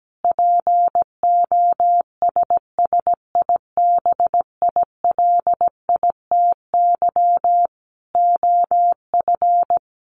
Morse code, heard as a keyed tone, about 17 wpm, 700 Hz. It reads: POSSIBILITY OF
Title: Sets of 3 Words 17wpm